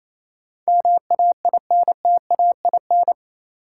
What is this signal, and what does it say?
Morse code, heard as a keyed tone, about 28 wpm, 700 Hz. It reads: MAINTAIN
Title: Sentences from Top 1000 Words - Review Part 5 - New Word Drill - 28wpm